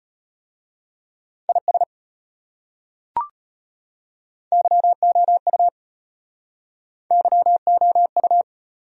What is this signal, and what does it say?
Morse code, heard as a keyed tone, about 34 wpm, 700 Hz. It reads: IS YOU YOU